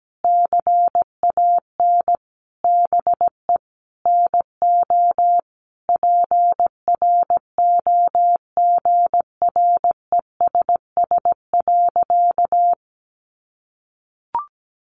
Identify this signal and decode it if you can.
Morse code, heard as a keyed tone, about 17 wpm, 700 Hz. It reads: CAN BE NO PROGRESS.
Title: Sentences from Top 1400 Words 17wpm